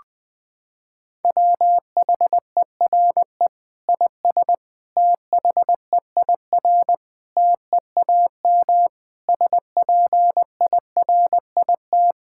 Morse code, heard as a keyed tone, about 20 wpm, 700 Hz. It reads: WHERE IS THEIR TEAM SPIRIT
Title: Sentences from Top 1600 Words 20wpm